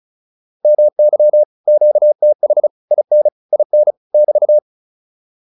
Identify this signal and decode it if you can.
Morse code, heard as a keyed tone, about 34 wpm, 600 Hz. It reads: MY QTH IN IN <BT>
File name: QSO205 35